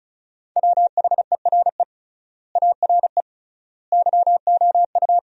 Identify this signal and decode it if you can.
Morse code, heard as a keyed tone, about 34 wpm, 700 Hz. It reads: WHERE ARE YOU
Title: 3x Word-Spacing - Sentences from Top 400 Words - 35wpm